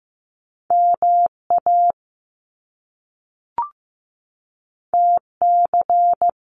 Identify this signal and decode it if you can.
Morse code, heard as a keyed tone, about 15 wpm, 700 Hz. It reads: MA TC
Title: Top Two Letter Combinations in Words 15wpm